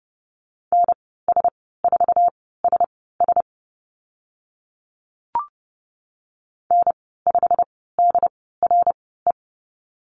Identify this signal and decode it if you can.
Morse code, heard as a keyed tone, about 30 wpm, spaced as for 17 wpm, 700 Hz. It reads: NS4SS N5DRE
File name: Lesson 15g - Number 5 - Call Signs 30wpm